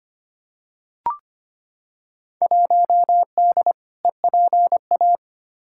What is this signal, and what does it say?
Morse code, heard as a keyed tone, about 25 wpm, 700 Hz. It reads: 1D EPA
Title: Field Day Exchanges - 25wpm